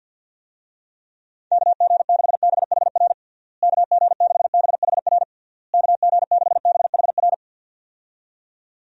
Morse code, heard as a keyed tone, about 51 wpm, 700 Hz. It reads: KG6BHR KG6BHR KG6BHR
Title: USA Call Signs - 6 Characters - Repeated 3x - 50wpm